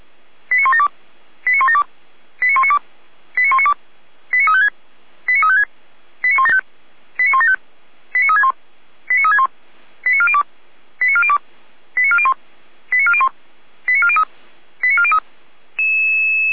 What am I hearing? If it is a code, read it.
Five-tone selective call (ZVEI), heard as quick runs of five tones, tones of about 70 ms. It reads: 89172 89172 89192 89192 89356 89356 89175 89175 89361 89361 89402 89402 89401 89401 89403 89403